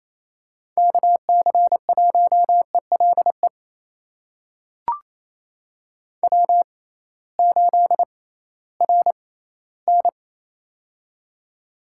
Morse code, heard as a keyed tone, about 28 wpm, spaced as for 16 wpm, 700 Hz. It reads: KC1ELE W8RN
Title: USA Call Signs - 6x Character Spacing - 28wpm